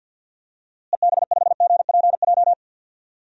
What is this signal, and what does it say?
Morse code, heard as a keyed tone, about 46 wpm, 700 Hz. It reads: EB5GPJ
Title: DXCC-100 Call Signs - 6 Characters - Repeated 2x - 50wpm